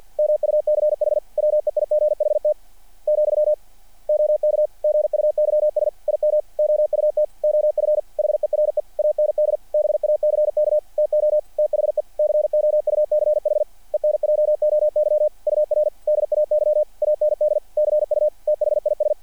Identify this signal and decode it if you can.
Morse code, heard as a keyed tone, about 48 wpm, 600 Hz. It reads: MUCH WEIGHT , OK GUYS IM OUT OF HERE AND BACK TO THE COUCH ENJOY UR DAY AND CU THIS
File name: archived / CRSnet